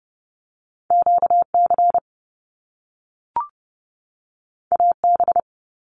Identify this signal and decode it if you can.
Morse code, heard as a keyed tone, about 30 wpm, 700 Hz. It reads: QC AB
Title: Canadian Provinces and Territories 30wpm